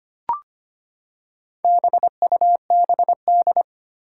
Morse code, heard as a keyed tone, about 25 wpm, 700 Hz. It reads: BUBD
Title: Four Challenging Letter-Number Combinations 25wpm